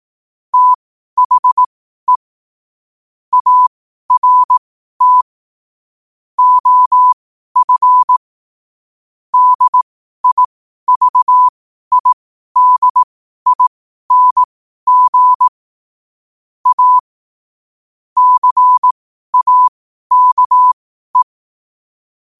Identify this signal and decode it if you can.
Morse code, heard as a keyed tone, about 18 wpm, spaced as for 12 wpm, 1000 Hz. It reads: THE ART OF DIVIDING A CAKE
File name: qotdfiles / 12wpm